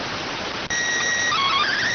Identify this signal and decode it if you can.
SSTV (slow-scan television), PD 290.